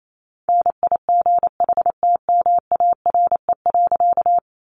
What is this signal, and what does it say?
Morse code, heard as a keyed tone, about 28 wpm, 700 Hz. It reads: NIGHTMARE.
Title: A Sound Of Thunder 28wpm